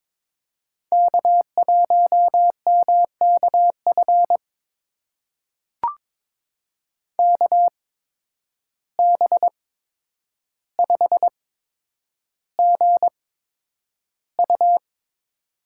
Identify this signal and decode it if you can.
Morse code, heard as a keyed tone, about 22 wpm, spaced as for 9 wpm, 700 Hz. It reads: K1MKF KB5GU
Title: USA Call Signs - 5 Characters - 8x Character Spacing - 22wpm